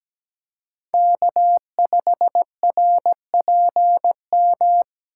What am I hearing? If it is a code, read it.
Morse code, heard as a keyed tone, about 17 wpm, 700 Hz. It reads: K5RPM
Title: USA Call Signs - 5 Characters - 2x Character Spacing - 17wpm